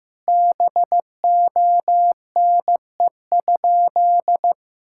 Morse code, heard as a keyed tone, about 15 wpm, 700 Hz. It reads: BONE?
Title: Sentences from Top 1600 Words 15wpm